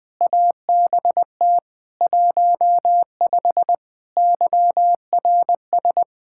Morse code, heard as a keyed tone, about 20 wpm, 700 Hz. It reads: ABT 15 YRS
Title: Practice QSO - Exchange 3 - More Info - 20wpm